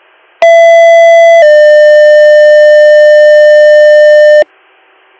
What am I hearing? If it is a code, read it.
Two-tone page, heard as a one-second tone, then a three-second tone, about 670 then 600 Hz, with noise. It is heard from 0.4 to 4.4 s.